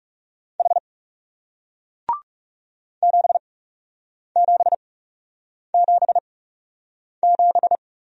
Morse code, heard as a keyed tone, about 40 wpm, spaced as for 8 wpm, 700 Hz. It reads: H 7777